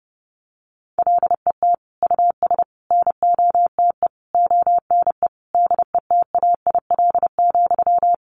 Morse code, heard as a keyed tone, about 30 wpm, 700 Hz. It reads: LET US NOTE ONE DETAIL,
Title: Sentences from Top 1500 Words 30wpm